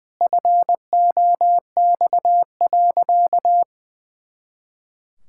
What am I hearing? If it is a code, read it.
Morse code, heard as a keyed tone, about 20 wpm, 700 Hz. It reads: FOX.
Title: Pangram Challenge v2 - 20wpm